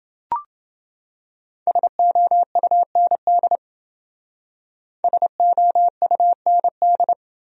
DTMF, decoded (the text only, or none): none